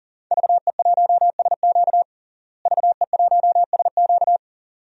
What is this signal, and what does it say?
Morse code, heard as a keyed tone, about 40 wpm, 700 Hz. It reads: VE1SQ VE1SQ
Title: DXCC-100 Call Signs - 5 Characters - Repeated 2x - 40wpm